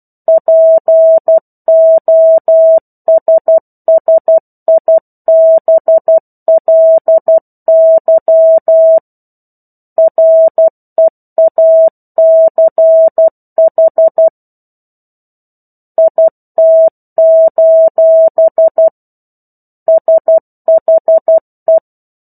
Morse code, heard as a keyed tone, about 12 wpm, 650 Hz. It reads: POSSIBLY REACH IT: SHE
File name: 01-Chapter-01 12 WPM